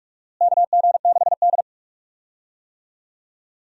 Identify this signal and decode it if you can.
Morse code, heard as a keyed tone, about 41 wpm, 700 Hz. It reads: KG6D
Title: USA Call Signs - 4 Characters - Repeated 2x - 45wpm